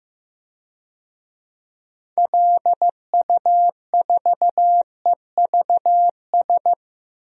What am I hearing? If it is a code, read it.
Morse code, heard as a keyed tone, about 15 wpm, 700 Hz. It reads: LU4EVS
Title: DXCC-100 Call Signs - 6 Characters - 15wpm